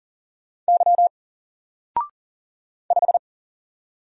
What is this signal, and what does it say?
Morse code, heard as a keyed tone, about 40 wpm, 700 Hz. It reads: Y 5
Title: Single Letter-Number Alt 40wpm